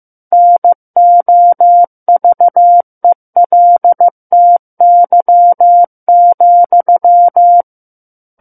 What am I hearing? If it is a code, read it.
Morse code, heard as a keyed tone, about 15 wpm, 700 Hz. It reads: NOVELTY,
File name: Chapter 6 - 15 WPM